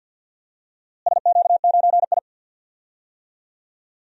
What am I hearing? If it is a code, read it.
Morse code, heard as a keyed tone, about 51 wpm, 700 Hz. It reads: SQ9I
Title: DXCC-100 Call Signs - 4 Characters - Repeated 3x - 50wpm